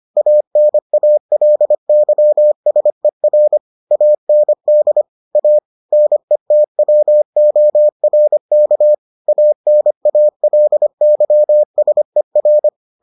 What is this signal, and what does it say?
Morse code, heard as a keyed tone, about 25 wpm, 600 Hz. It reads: ANALYSER AND A NETWORK ANALYSER